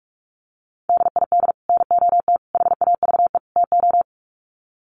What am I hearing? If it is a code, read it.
Morse code, heard as a keyed tone, about 44 wpm, 700 Hz. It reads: DID NOT HAVE TO
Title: Sets of 4 Words 45wpm